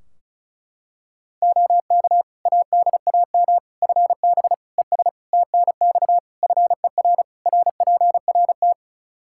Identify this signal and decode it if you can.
Morse code, heard as a keyed tone, about 36 wpm, 700 Hz. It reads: OK ADAM FB ES TNX FER RPRT
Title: Practice QSO - Exchange 2 - Info - 35wpm